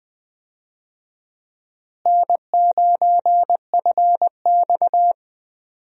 Morse code, heard as a keyed tone, about 20 wpm, 700 Hz. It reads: N9FX
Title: USA Call Signs - 4 Characters - Repeated 2x - 20wpm